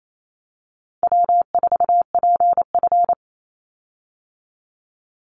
Morse code, heard as a keyed tone, about 28 wpm, 700 Hz. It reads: W4PF
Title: Easy Call Signs - 28wpm